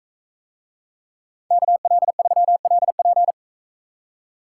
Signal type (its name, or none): Morse code